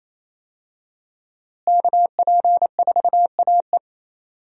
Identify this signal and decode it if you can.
Morse code, heard as a keyed tone, about 28 wpm, 700 Hz. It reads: KP4AE
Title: USA Call Signs - 5 Characters - Repeated 2x - 28wpm